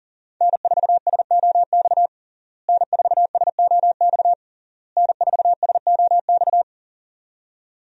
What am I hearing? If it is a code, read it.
Morse code, heard as a keyed tone, about 37 wpm, 700 Hz. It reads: N4SOX N4SOX N4SOX
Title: USA Call Signs - 5 Characters - Repeated 3x - 40wpm